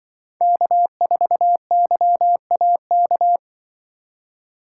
Morse code, heard as a keyed tone, about 25 wpm, 700 Hz. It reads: K4YAK
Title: Speed Racing - USA 1x3 Call Signs 20wpm